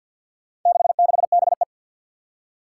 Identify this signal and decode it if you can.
Morse code, heard as a keyed tone, about 48 wpm, 700 Hz. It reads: 66BE
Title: Four Challenging Letter-Number Combinations 50wpm